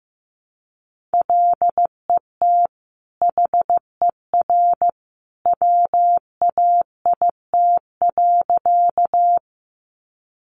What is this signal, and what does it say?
Morse code, heard as a keyed tone, about 15 wpm, 700 Hz. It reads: LET HER WAIT.
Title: Sentences from Top 500 Words 15wpm